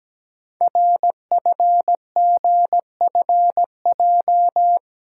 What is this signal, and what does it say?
Morse code, heard as a keyed tone, about 17 wpm, 700 Hz. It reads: RFGFJ
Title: Code Groups - 5 Letters - 17wpm